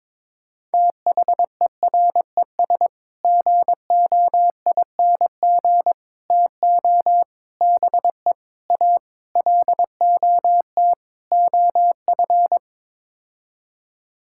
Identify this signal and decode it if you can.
Morse code, heard as a keyed tone, about 22 wpm, 700 Hz. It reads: THERES GOING TO BE A LOT OF
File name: practice / 22wpm / Sets of 7 Words - 22wpm